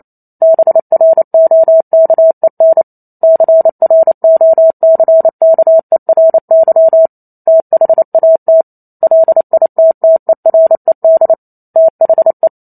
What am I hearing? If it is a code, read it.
Morse code, heard as a keyed tone, about 28 wpm, 650 Hz. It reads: BROKEN CROCKERY THAT LITTERED THE